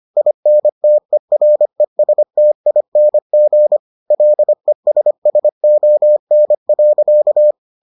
Morse code, heard as a keyed tone, about 25 wpm, 600 Hz. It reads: INTERESTING LESSON.